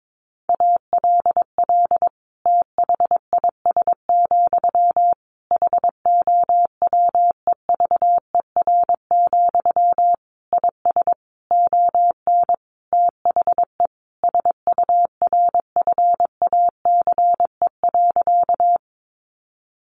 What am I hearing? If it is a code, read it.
Morse code, heard as a keyed tone, about 22 wpm, 700 Hz. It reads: ALL THIS, HOWEVER, IS ON THE SURFACE.
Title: Sentences from Top 600 Words 22wpm